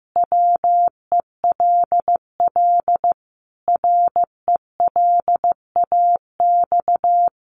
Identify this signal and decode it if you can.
Morse code, heard as a keyed tone, about 15 wpm, 700 Hz. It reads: WELL RELAX
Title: Sentences from Top 1400 Words 15wpm